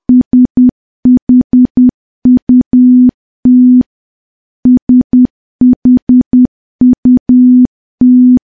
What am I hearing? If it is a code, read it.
Morse code, heard as a keyed tone, about 10 wpm, 250 Hz. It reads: SHUT SHUT
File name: morse / morse-training-10WPM